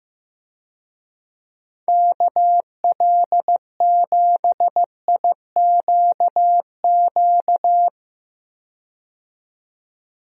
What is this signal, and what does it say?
Morse code, heard as a keyed tone, about 15 wpm, 700 Hz. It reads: KL7IQQ
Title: DXCC-100 Call Signs - 6 Characters - 15wpm